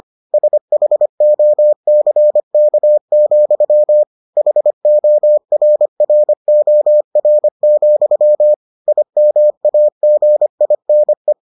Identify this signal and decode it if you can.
Morse code, heard as a keyed tone, about 25 wpm, 600 Hz. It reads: SHOCK, HORROR, IMAGINE